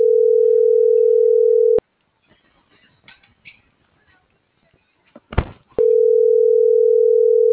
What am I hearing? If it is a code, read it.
Ringback tone (North America), 440 + 480 Hz.